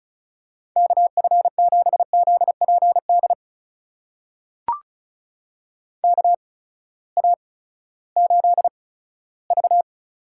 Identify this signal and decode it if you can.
Morse code, heard as a keyed tone, about 34 wpm, spaced as for 17 wpm, 700 Hz. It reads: KF7ZPD KA8V
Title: USA Call Signs - 6 Characters - 8x Character Spacing - 35wpm